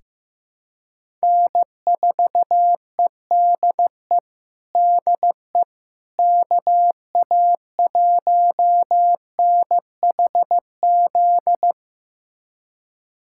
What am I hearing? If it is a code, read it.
Morse code, heard as a keyed tone, about 15 wpm, 700 Hz. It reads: N4EDE DE KA1NHZ
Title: USA Callsign DE Callsign 15wpm